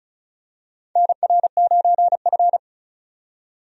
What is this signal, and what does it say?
Morse code, heard as a keyed tone, about 36 wpm, 700 Hz. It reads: NR9F
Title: USA Call Signs - 4 Characters - 35wpm